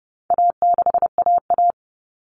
Morse code, heard as a keyed tone, about 30 wpm, 700 Hz. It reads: A6AA
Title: All Cut Numbers - 4-Digit 30wpm